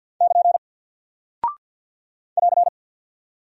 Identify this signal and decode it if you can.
Morse code, heard as a keyed tone, about 48 wpm, 700 Hz. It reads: <KN> <AR>
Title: Single Character (Letter:Number:Punctuation:Prosign) Alt 50wpm